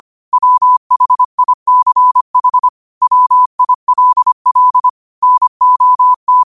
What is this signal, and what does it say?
Morse code, heard as a keyed tone, about 25 wpm, 1000 Hz. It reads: WHICH WILL NOT
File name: qotdfiles / 25wpm